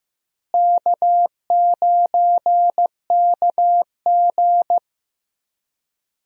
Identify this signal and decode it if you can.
Morse code, heard as a keyed tone, about 15 wpm, 700 Hz. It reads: K9KG
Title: USA Call Signs - 4 Characters - Repeated 3x - 15wpm